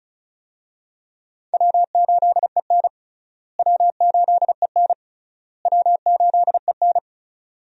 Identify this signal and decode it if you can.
Morse code, heard as a keyed tone, about 36 wpm, 700 Hz. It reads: W8EN W8EN W8EN
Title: USA Call Signs - 4 Characters - Repeated 3x - 35wpm